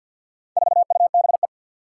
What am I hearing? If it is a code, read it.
Morse code, heard as a keyed tone, about 52 wpm, 700 Hz. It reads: 4UBE
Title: Four Challenging Letter-Number Combinations 50wpm